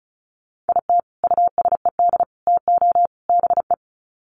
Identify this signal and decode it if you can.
Morse code, heard as a keyed tone, about 34 wpm, 700 Hz. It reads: IT USED TO BE
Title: Sets of 4 Words 35wpm